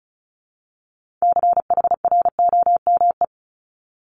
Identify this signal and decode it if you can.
Morse code, heard as a keyed tone, about 36 wpm, 700 Hz. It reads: CHROME
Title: Multi-letter Phonemes 35wpm